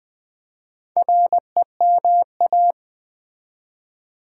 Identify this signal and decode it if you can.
Morse code, heard as a keyed tone, about 20 wpm, 700 Hz. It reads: REMA
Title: Build A Word - 4 Letters At A Time - 10-Letter Words - 20wpm